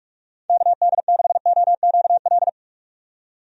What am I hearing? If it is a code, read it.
Morse code, heard as a keyed tone, about 46 wpm, 700 Hz. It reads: KD6OQL
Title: USA Call Signs - 6 Characters - 45wpm